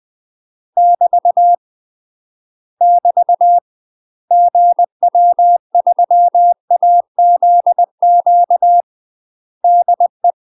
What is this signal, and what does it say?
Morse code, heard as a keyed tone, about 20 wpm, 700 Hz. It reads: <BT> <BT> GW3AZQ DE
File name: UK MASTER QSOs TRAINING/20WPM